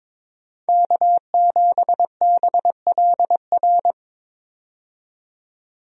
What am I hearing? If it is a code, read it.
Morse code, heard as a keyed tone, about 22 wpm, 700 Hz. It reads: K7BLR
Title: USA Call Signs - 5 Characters - 22wpm